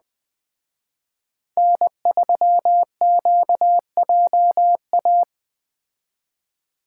Morse code, heard as a keyed tone, about 21 wpm, 700 Hz. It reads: N3QJA
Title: Speed Racing - USA 1x3 Call Signs 17wpm